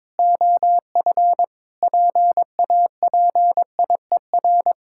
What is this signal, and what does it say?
Morse code, heard as a keyed tone, about 22 wpm, 700 Hz. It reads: OF PAPIER
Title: Pangram Challenge v2 - 22wpm